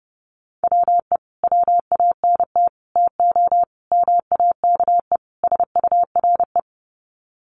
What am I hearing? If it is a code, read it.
Morse code, heard as a keyed tone, about 30 wpm, 700 Hz. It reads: WE WANT TO MAKE SURE
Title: Sets of 5 Words Encore2 30wpm